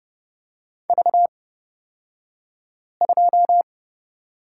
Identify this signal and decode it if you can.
Morse code, heard as a keyed tone, about 30 wpm, spaced as for 5 wpm, 700 Hz. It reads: V2
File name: Lesson 37d - Letter x - Two Characters 30wpm